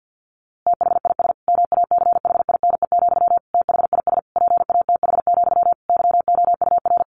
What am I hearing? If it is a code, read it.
Morse code, heard as a keyed tone, about 50 wpm, 700 Hz. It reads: THIS MACHINE, THIS PATH, YOUR
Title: A Sound Of Thunder 50wpm